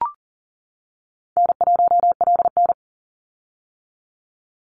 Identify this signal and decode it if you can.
Morse code, heard as a keyed tone, about 40 wpm, 700 Hz. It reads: N1LN
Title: Easy Call Signs - 40wpm